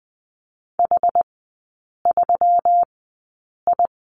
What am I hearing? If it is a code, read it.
Morse code, heard as a keyed tone, about 20 wpm, spaced as for 8 wpm, 700 Hz. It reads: H3I
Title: 159 Lesson 26e - Letter g - Three Characters 20wpm